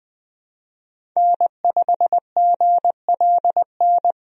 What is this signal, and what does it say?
Morse code, heard as a keyed tone, about 20 wpm, 700 Hz. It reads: N5GLN
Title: USA Call Signs - 5 Characters - 20wpm